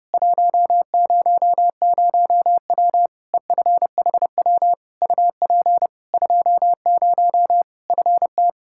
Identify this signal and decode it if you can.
Morse code, heard as a keyed tone, about 32 wpm, 700 Hz. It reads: 100W EFHW UP 20 FT
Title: K1USN SST QSO Practice 30wpm